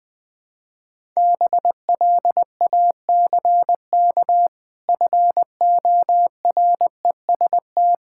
Morse code, heard as a keyed tone, about 19 wpm, 700 Hz. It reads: BLACK FOREST
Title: Difficult Words with Hints - 20wpm